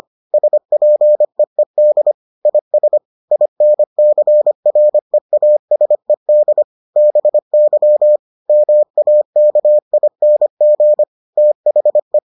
Morse code, heard as a keyed tone, about 25 wpm, 600 Hz. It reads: SPEED IS INCREASED BY MAKING THE